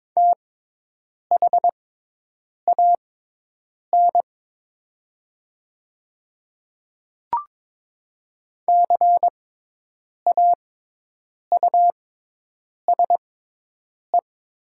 Morse code, heard as a keyed tone, about 22 wpm, spaced as for 8 wpm, 700 Hz. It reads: THAN CAUSE
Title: 6x Character Spacing - Top 200 Words - 22wpm